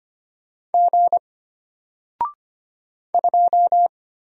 Morse code, heard as a keyed tone, about 25 wpm, 700 Hz. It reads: G 2
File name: Single-Letter-Number-25wpm